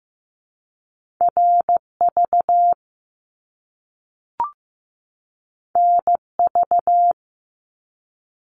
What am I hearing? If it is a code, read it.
Morse code, heard as a keyed tone, about 15 wpm, 700 Hz. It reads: RV NV
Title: Top Two Letter Combinations in Words 15wpm